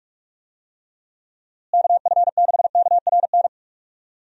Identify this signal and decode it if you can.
Morse code, heard as a keyed tone, about 48 wpm, 700 Hz. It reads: KF6KRN
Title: USA Call Signs - 6 Characters - Repeated 2x - 45wpm